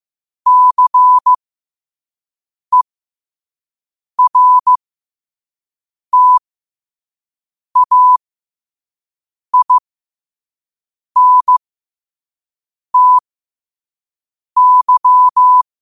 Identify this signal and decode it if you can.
Morse code, heard as a keyed tone, about 15 wpm, spaced as for 5 wpm, 1000 Hz. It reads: CERTAINTY
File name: qotdfiles / 5wpm